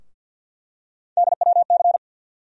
Morse code, heard as a keyed tone, about 51 wpm, 700 Hz. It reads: DWC
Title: USA Call Sign Postfixes - 3 Letters 50wpm